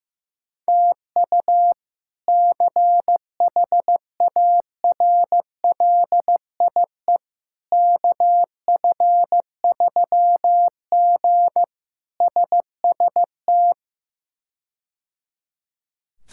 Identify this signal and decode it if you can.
Morse code, heard as a keyed tone, about 15 wpm, 700 Hz. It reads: TU CHARLIE KF3G SST
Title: K1USN SST QSO Practice - 15wpm